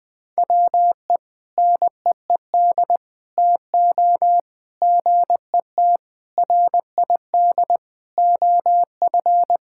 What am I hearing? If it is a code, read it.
Morse code, heard as a keyed tone, about 20 wpm, 700 Hz. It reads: WE NEED TO GET RID OF
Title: Sentences with 4-Word Phrases - 20wpm